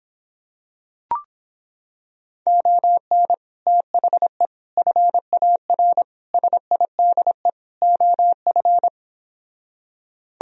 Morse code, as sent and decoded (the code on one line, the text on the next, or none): --- -. / - .... / ..-. .- .-. / ... .. -.. / --- ..-.
ON THE FAR SIDE OF